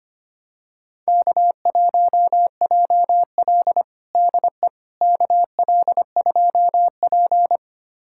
Morse code, heard as a keyed tone, about 25 wpm, 700 Hz. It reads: K1JL DE KL2P
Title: USA Callsign DE Callsign - 4 Characters - 25wpm